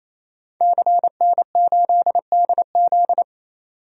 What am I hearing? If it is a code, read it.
Morse code, heard as a keyed tone, about 28 wpm, 700 Hz. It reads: CN8DZ
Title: DXCC-100 Call Signs - 5 Characters - 28wpm